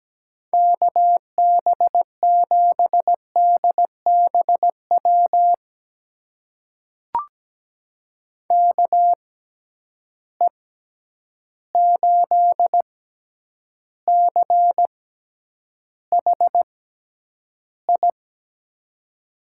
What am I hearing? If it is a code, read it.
Morse code, heard as a keyed tone, about 17 wpm, spaced as for 9 wpm, 700 Hz. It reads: KB7DBW KE8CHI
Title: USA Call Signs - 6 Characters - 6x Character Spacing - 17wpm